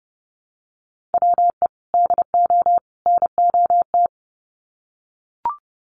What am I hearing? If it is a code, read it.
Morse code, heard as a keyed tone, about 30 wpm, 700 Hz. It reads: WE DO NOT
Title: Sets of 3 Words 30wpm